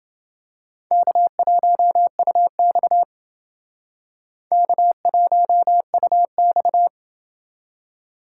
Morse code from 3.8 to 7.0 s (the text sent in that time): K1UX